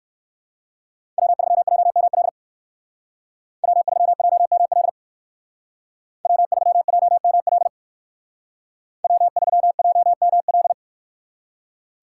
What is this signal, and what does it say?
Morse code, heard as a keyed tone, about 58 wpm, 700 Hz. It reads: W3JML W3JML W3JML W3JML